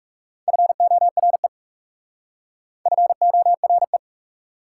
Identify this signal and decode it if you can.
Morse code, heard as a keyed tone, about 39 wpm, 700 Hz. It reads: FORE FORE